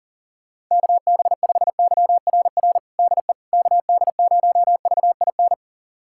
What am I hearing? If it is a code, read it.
Morse code, heard as a keyed tone, about 40 wpm, 700 Hz. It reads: KB5YRR DE KD0VIN
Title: USA Callsign DE Callsign 40wpm